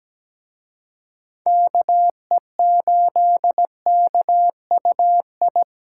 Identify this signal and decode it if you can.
Morse code, heard as a keyed tone, about 20 wpm, 700 Hz. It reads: KE8KUI
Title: Speed Racing - USA 2x3 Call Signs 17wpm